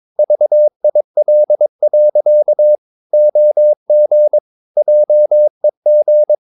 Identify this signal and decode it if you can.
Morse code, heard as a keyed tone, about 22 wpm, 600 Hz. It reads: VIL. OG JEG
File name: Ferie QRS 22wpm